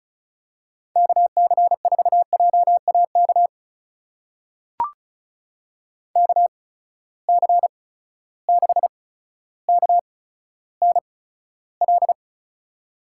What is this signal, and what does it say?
Morse code, heard as a keyed tone, about 36 wpm, spaced as for 15 wpm, 700 Hz. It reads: KC4JAK KC6KNL
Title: USA Call Signs - 6 Characters - 8x Character Spacing - 35wpm